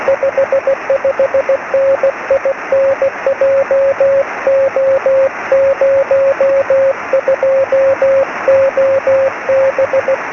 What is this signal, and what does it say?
Morse code, heard as a keyed tone, about 16 wpm, 550 Hz, with radio noise. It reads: 55N IN JO02OB
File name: sm7lcb / g3xdy 23cm